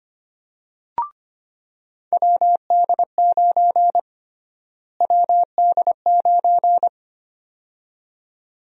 Morse code, heard as a keyed tone, about 25 wpm, 700 Hz. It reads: WD9 WD9
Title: USA Call Sign Prefixes - Repeated 2x - 25wpm